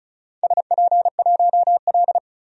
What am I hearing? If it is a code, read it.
Morse code, heard as a keyed tone, about 36 wpm, 700 Hz. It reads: SP1L
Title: DXCC-100 Call Signs - 4 Characters - 35wpm